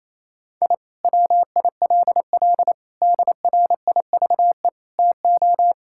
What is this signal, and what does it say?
Morse code, heard as a keyed tone, about 27 wpm, 700 Hz. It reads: I WILL DRIVE TO
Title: Sentences from Top 1000 Words - Review Part 6 - 28wpm